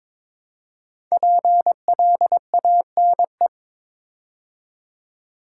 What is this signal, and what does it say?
Morse code, heard as a keyed tone, about 22 wpm, 700 Hz. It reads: PLANE
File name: 5 Letter Words - First 3 Letters Given - 22wpm